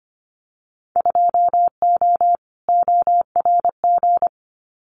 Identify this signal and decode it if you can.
Morse code, heard as a keyed tone, about 25 wpm, 700 Hz. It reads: 2O ORG
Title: Winter Field Day Exchanges 25wpm